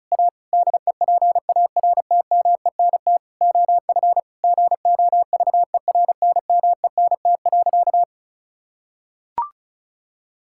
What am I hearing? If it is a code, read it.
Morse code, heard as a keyed tone, about 34 wpm, 700 Hz. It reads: A DEPARTMENT OF GOVERNMENT.
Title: Sentences from Top 1000 Words - Review Part 5 - 35wpm